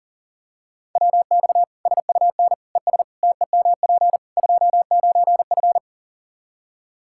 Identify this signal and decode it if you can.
Morse code, heard as a keyed tone, about 40 wpm, 700 Hz. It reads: WX SUN ES TEMP 29F
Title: Practice QSO - Exchange 2 - Info - 40wpm